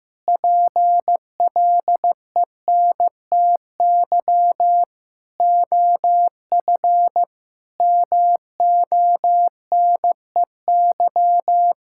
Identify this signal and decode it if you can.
Morse code, heard as a keyed tone, about 15 wpm, 700 Hz. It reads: PLENTY OF MONEY
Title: Sentences from Top 1600 Words 15wpm